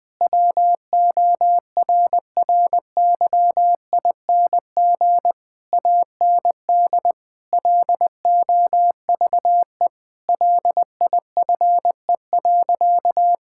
Morse code, heard as a keyed tone, about 21 wpm, 700 Hz. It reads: WORRYING AND LOVE LIFE.
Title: Sentences with 8-Word Phrases - 20wpm